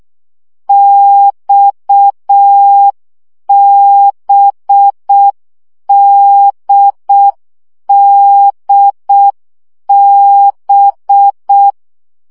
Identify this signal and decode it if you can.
Morse code, heard as a keyed tone, about 6 wpm, 800 Hz. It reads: XBDDB
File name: CW gyakorlas d b x 5 baud